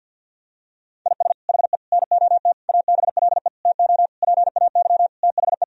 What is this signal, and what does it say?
Morse code, heard as a keyed tone, about 50 wpm, 700 Hz. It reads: IS HE NOT ABLE TO PAY THE
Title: Sentences from Top 1000 Words - Review Part 2 - 50wpm